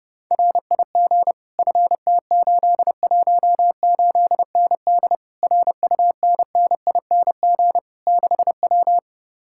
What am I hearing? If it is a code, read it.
Morse code, heard as a keyed tone, about 31 wpm, 700 Hz. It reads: RIG FT818ND RUNNING 6W